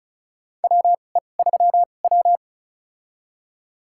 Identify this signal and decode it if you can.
Morse code, heard as a keyed tone, about 34 wpm, spaced as for 25 wpm, 700 Hz. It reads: WE3W
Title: USA Call Signs - 4 Characters - 2x Character Spacing - 35wpm